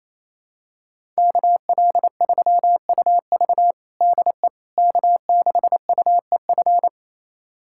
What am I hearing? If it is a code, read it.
Morse code, heard as a keyed tone, about 28 wpm, 700 Hz. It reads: KL3UV DE K6UEF
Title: USA Callsign DE Callsign - 5 Characters - 28wpm